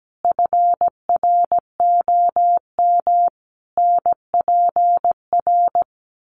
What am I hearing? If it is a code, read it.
Morse code, heard as a keyed tone, about 17 wpm, 700 Hz. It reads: FROM NPR
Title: Sets of 5 Words Encore 17wpm